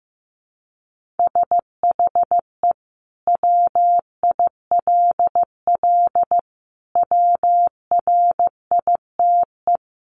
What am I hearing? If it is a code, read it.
Morse code, heard as a keyed tone, about 15 wpm, 700 Hz. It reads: SHE WILL WRITE
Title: Sentences from Top 1500 Words 15wpm